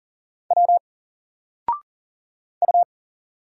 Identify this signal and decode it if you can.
Morse code, heard as a keyed tone, about 40 wpm, 700 Hz. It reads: W U
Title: Single Letter-Number Alt 40wpm